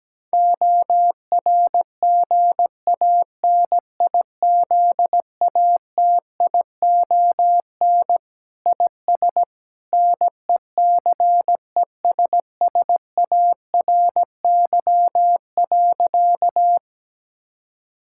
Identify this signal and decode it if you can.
Morse code, heard as a keyed tone, about 17 wpm, 700 Hz. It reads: ORGANIZATION IS NECESSARY.
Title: Sentences from Top 1000 Words - Review Part 5 - 17wpm